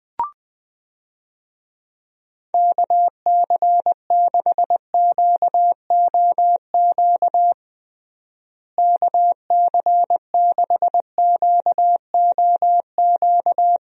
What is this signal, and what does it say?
Morse code, heard as a keyed tone, about 20 wpm, 700 Hz. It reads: KC6QOQ KC6QOQ
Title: USA Call Signs - 6 Characters - Repeated 3x - 20wpm